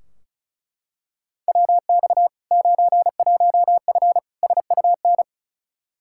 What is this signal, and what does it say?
Morse code, heard as a keyed tone, about 36 wpm, 700 Hz. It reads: WX 91F SUN
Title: Practice QSO - Exchange 2 - Info - 35wpm